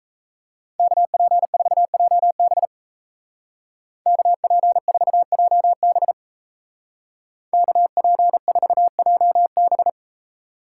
Morse code, heard as a keyed tone, about 37 wpm, 700 Hz. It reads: KP4JB KP4JB KP4JB